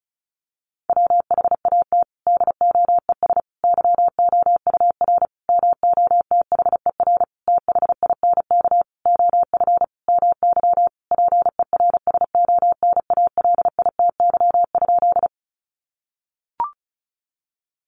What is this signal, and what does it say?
Morse code, heard as a keyed tone, about 34 wpm, 700 Hz. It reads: WHAT DOES YOUR MOTHER THINK OF MY PERSONALITY?
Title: Sentences from Top 1400 Words 35wpm